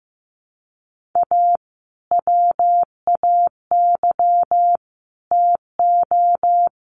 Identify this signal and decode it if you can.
Morse code, heard as a keyed tone, about 15 wpm, 700 Hz. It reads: A WAY TO
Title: Sets of 3 Words 15wpm